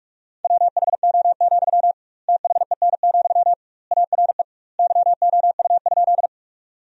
Morse code, heard as a keyed tone, about 46 wpm, 700 Hz. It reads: WHO, THEN, ARE YOU?
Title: 2x Word-Spacing - Sentences from Top 100 Words - 45wpm